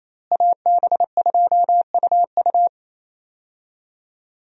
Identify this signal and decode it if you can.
Morse code, heard as a keyed tone, about 28 wpm, 700 Hz. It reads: AB2UU
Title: USA Call Signs - 5 Characters - Repeated 3x - 28wpm